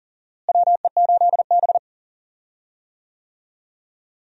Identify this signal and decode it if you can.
Morse code, heard as a keyed tone, about 40 wpm, 700 Hz. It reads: WE8B
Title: USA Call Signs - 4 Characters - Repeated 2x - 40wpm